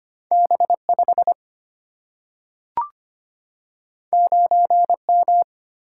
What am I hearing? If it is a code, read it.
Morse code, heard as a keyed tone, about 25 wpm, 700 Hz. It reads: B 5 9 M
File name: Two Letter-Number Combinations - 25wpm